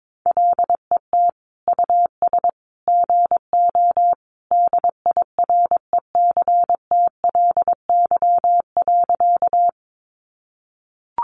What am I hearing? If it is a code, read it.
Morse code, heard as a keyed tone, about 22 wpm, 700 Hz. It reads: LET US GO DIRECTLY.
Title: Sentences from Top 800 Words 22wpm